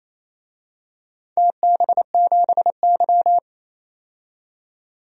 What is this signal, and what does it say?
Morse code, heard as a keyed tone, about 28 wpm, 700 Hz. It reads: TB7Y
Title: DXCC-100 Call Signs - 4 Characters - 28wpm